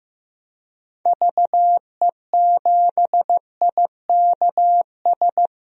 Morse code, heard as a keyed tone, about 15 wpm, 700 Hz. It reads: VE7IKS
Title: DXCC-100 Call Signs - 6 Characters - 15wpm